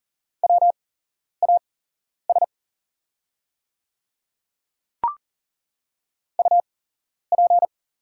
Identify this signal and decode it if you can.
Morse code, heard as a keyed tone, about 40 wpm, spaced as for 11 wpm, 700 Hz. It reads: WAS UP
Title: 8x Character Spacing - Top 200 Words - 40wpm